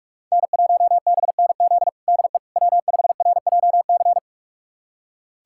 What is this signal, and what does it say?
Morse code, heard as a keyed tone, about 43 wpm, 700 Hz. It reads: N1BNZ DE W5RJC
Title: USA Callsign DE Callsign - 5 Characters - 45wpm